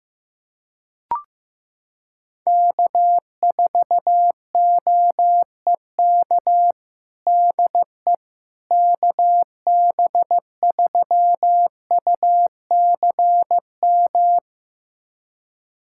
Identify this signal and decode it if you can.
Morse code, heard as a keyed tone, about 15 wpm, 700 Hz. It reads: K4OEK DE KB3UCM
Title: USA Callsign DE Callsign 15wpm